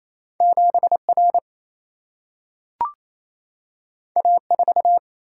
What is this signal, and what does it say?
Morse code, heard as a keyed tone, about 28 wpm, 700 Hz. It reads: 7 R A 4
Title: Single-Letter Single-Number 28wpm